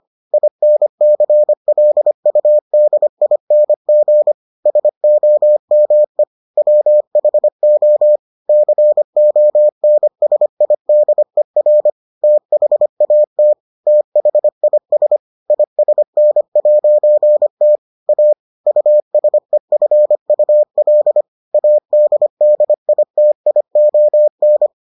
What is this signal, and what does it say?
Morse code, heard as a keyed tone, about 25 wpm, 600 Hz. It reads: INCLUDING SOME WHO CONSIDER THAT THIS ISN'T A USEFUL ADDITION